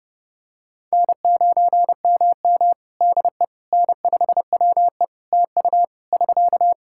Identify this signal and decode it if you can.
Morse code, heard as a keyed tone, about 29 wpm, 700 Hz. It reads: N9MM DE N5WE TU <SK>
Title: Practice QSO - Exchange 4 - Ending - 30wpm